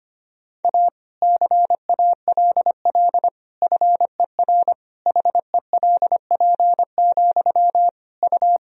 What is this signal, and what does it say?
Morse code, heard as a keyed tone, about 25 wpm, 700 Hz. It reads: A CALL FER HELP, U
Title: Sentences from Top 300 Words and Abbreviations 25wpm